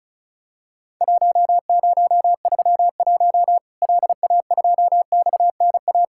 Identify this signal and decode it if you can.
Morse code, heard as a keyed tone, about 36 wpm, 700 Hz. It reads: 1031 LA2XNA
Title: Worked All Europe DX Contest - QTC Exchange - 35wpm